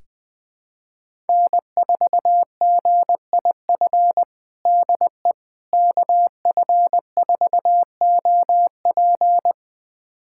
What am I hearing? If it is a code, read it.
Morse code, heard as a keyed tone, about 20 wpm, 700 Hz. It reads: N4GIF DE KF4OP
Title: USA Callsign DE Callsign - 5 Characters - 20wpm